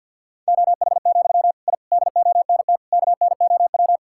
Morse code, heard as a keyed tone, about 50 wpm, 700 Hz. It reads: OH, I DONT KNOW
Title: Sentences from Top 1000 Words - Review Part 6 - 50wpm